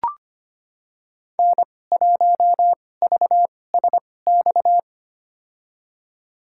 Morse code, heard as a keyed tone, about 25 wpm, spaced as for 18 wpm, 700 Hz. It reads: N1VSX
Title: USA Call Signs - 2x Character Spacing - 25wpm